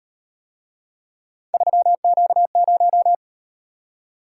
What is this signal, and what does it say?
Morse code, heard as a keyed tone, about 37 wpm, 700 Hz. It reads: 3Q0